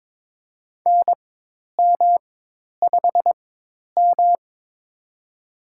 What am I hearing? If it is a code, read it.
Morse code, heard as a keyed tone, about 21 wpm, spaced as for 10 wpm, 700 Hz. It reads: NM5M
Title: USA Call Signs - 4 Characters - 4x Character Spacing - 22wpm